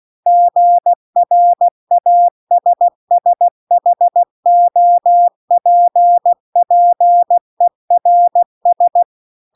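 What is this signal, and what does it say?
Morse code, heard as a keyed tone, about 16 wpm, 700 Hz. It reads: GRASSHOPPERS
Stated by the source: Morse practice text at 16 WPM.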